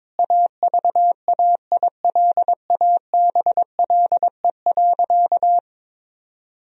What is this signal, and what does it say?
Morse code, heard as a keyed tone, about 23 wpm, 700 Hz. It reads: AVAILABLE.
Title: Sentences from Top 1000 Words - Review Part 3 - 22wpm